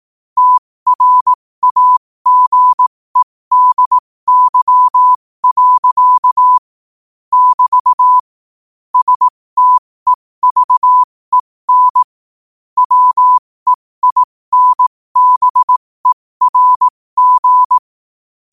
Morse code, heard as a keyed tone, about 18 wpm, spaced as for 15 wpm, 1000 Hz. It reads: TRAGEDY. <BT> STEVEN WEINBERG
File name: qotdfiles / 15wpm